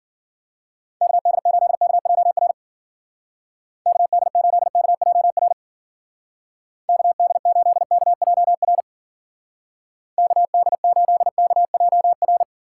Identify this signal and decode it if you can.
Morse code, heard as a keyed tone, about 52 wpm, 700 Hz. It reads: KD8KJR KD8KJR KD8KJR KD8KJR